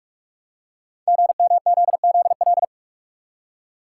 Morse code, heard as a keyed tone, about 44 wpm, 700 Hz. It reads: GMZZL
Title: Code Groups - 5 Letters - 45wpm